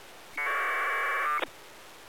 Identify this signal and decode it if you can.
Packet radio (AX.25, as in APRS).